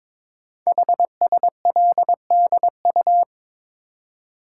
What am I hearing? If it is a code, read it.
Morse code, heard as a keyed tone, about 22 wpm, 700 Hz. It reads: HSLDU
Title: Code Groups - 5 Letter:Numbers - 22wpm